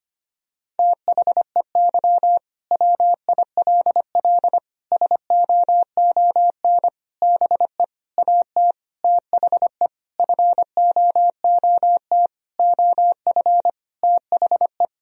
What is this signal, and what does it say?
Morse code, heard as a keyed tone, about 25 wpm, 700 Hz. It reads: THEY WILL SOON BE AT THE FOOT OF THE
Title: Sentences from Top 1000 Words - Review Part 4 - 25wpm